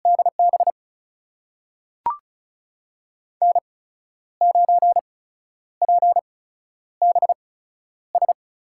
Morse code, heard as a keyed tone, about 36 wpm, spaced as for 11 wpm, 700 Hz. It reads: DB N9PBS
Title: USA Call Signs - 8x Character Spacing - 35wpm